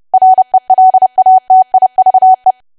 Morse code, heard as a keyed tone, about 30 wpm, 750 Hz. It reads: RELATIVE
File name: cwa / W1AW-Level3 / 30WPM